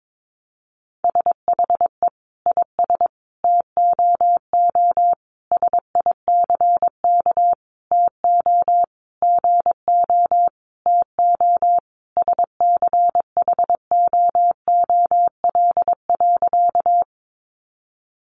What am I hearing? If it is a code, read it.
Morse code, heard as a keyed tone, about 22 wpm, 700 Hz. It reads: SHE IS TOO SICK TO GO TO SCHOOL.
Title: Easy Phrases 22wpm